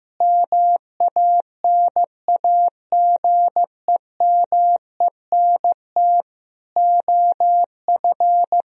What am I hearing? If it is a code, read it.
Morse code, heard as a keyed tone, about 15 wpm, 700 Hz. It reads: MANAGEMENT OF
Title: Sentences from Top 1000 Words - Review Part 4 - 15wpm